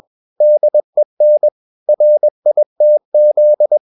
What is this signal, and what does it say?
Morse code, heard as a keyed tone, about 21 wpm, 600 Hz. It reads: DEN RITZ